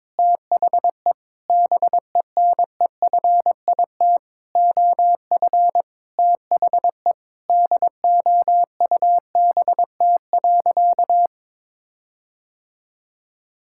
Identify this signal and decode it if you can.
Morse code, heard as a keyed tone, about 22 wpm, 700 Hz. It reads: THE BENEFIT OF THE DOUBT.
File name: Sentences with 7-Word Phrases - 22wpm